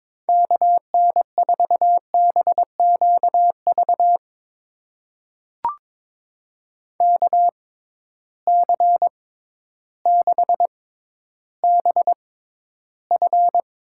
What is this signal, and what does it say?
Morse code, heard as a keyed tone, about 22 wpm, spaced as for 12 wpm, 700 Hz. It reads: KN4BQV KC6BF
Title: USA Call Signs - 6 Characters - 6x Character Spacing - 22wpm